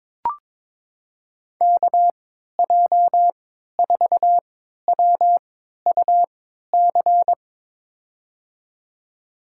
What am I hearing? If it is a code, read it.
Morse code, heard as a keyed tone, about 21 wpm, spaced as for 12 wpm, 700 Hz. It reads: KJ4WUC
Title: USA Call Signs - 6 Characters - 3x Character Spacing - 22wpm